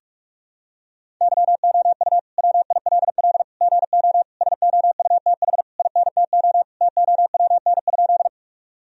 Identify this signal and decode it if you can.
Morse code, heard as a keyed tone, about 42 wpm, 700 Hz. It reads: YOU WILL GO SOUTH INTO TOWN?
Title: Sentences from Top 1000 Words - Review Part 2 - 45wpm